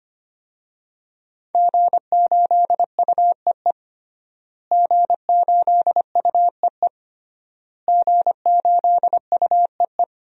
Morse code, heard as a keyed tone, about 25 wpm, 700 Hz. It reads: G8UEE G8UEE G8UEE
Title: DXCC-100 Call Signs - 5 Characters - Repeated 3x - 25wpm